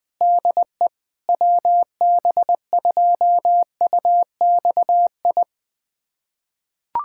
Morse code, heard as a keyed tone, about 20 wpm, 700 Hz. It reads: DE WB2UXI
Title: USA Callsign DE Callsign 20wpm